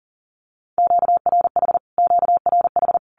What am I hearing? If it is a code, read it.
Morse code, heard as a keyed tone, about 40 wpm, 700 Hz. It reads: QRH QRH
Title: Q-Codes Repeated 3x 40wpm